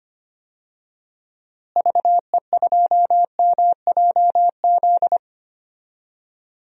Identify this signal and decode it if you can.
Morse code, heard as a keyed tone, about 25 wpm, 700 Hz. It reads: VE2MJZ
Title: DXCC-100 Call Signs - 6 Characters - Repeated 2x - 25wpm